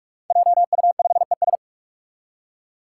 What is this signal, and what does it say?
Morse code, heard as a keyed tone, about 46 wpm, 700 Hz. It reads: JU5ES
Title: Five Challenging Letter-Number Combinations 45wpm